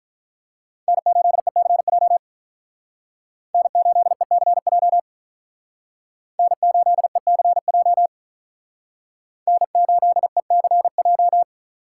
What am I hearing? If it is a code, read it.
Morse code, heard as a keyed tone, about 45 wpm, 700 Hz. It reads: N8ECJ N8ECJ N8ECJ N8ECJ